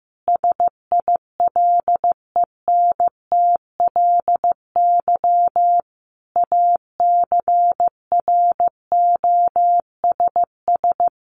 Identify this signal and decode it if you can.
Morse code, heard as a keyed tone, about 15 wpm, 700 Hz. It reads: SILENTLY ACROSS
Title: A Sound Of Thunder 15wpm